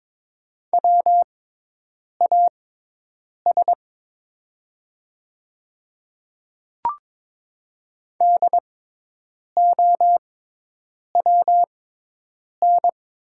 Morse code, heard as a keyed tone, about 23 wpm, spaced as for 8 wpm, 700 Hz. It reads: WAS DOWN
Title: 6x Character Spacing - Top 100 Words - 22wpm